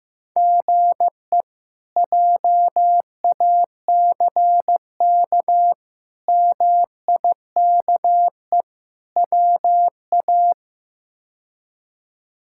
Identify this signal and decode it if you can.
Morse code, heard as a keyed tone, about 16 wpm, 700 Hz. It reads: GE JACK MIKE WA
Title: K1USN SST QSO Practice - 15wpm